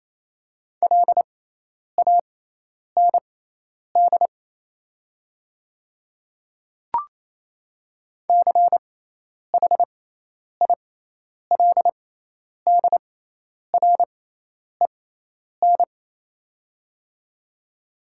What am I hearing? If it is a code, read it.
Morse code, heard as a keyed tone, about 28 wpm, spaced as for 10 wpm, 700 Hz. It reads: LAND CHILDREN
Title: 6x Character Spacing - Top 300 Words - 28wpm